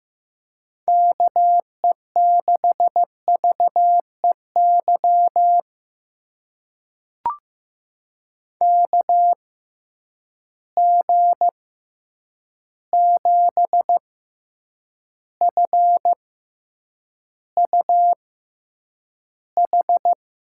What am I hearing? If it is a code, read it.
Morse code, heard as a keyed tone, about 15 wpm, spaced as for 8 wpm, 700 Hz. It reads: KE6VEY KG7FUH
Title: USA Call Signs - 6 Characters - 6x Character Spacing - 15wpm